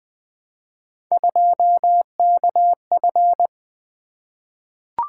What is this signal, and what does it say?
Morse code, heard as a keyed tone, about 20 wpm, 700 Hz. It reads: 2KF
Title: Three Letter-Number Combinations - 20wpm